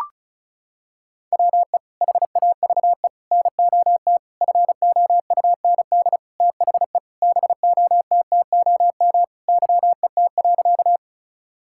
Morse code, heard as a keyed tone, about 36 wpm, 700 Hz. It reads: WE HAVE NOT FOUND THE BOTTOM YET.
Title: Sentences from Top 1000 Words - Review Part 6 - 35wpm